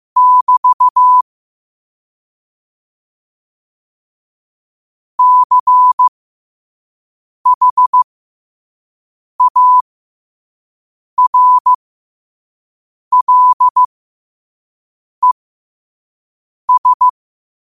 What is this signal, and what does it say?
Morse code, heard as a keyed tone, about 15 wpm, spaced as for 5 wpm, 1000 Hz. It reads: <BT> CHARLES
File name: qotdfiles / 5wpm